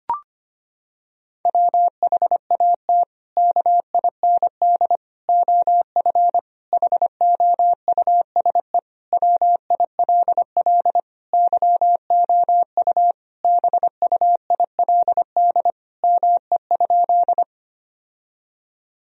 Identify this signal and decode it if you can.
Morse code, heard as a keyed tone, about 25 wpm, 700 Hz. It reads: WHAT KIND OF HOUSE WILL YOU BUILD ME?
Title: Sentences from Top 1000 Words - Review Part 1 - 25wpm